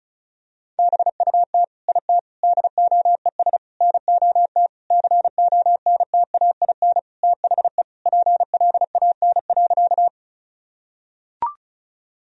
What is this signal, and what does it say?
Morse code, heard as a keyed tone, about 34 wpm, 700 Hz. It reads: BUT IT DOES NOT CONTAIN THE PLAN.
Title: Sentences from Top 1000 Words - Review Part 2 - 35wpm